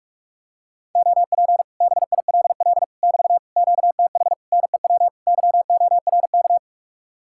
Morse code, heard as a keyed tone, about 40 wpm, 700 Hz. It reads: OP BILL <BT> QTH NEW YORK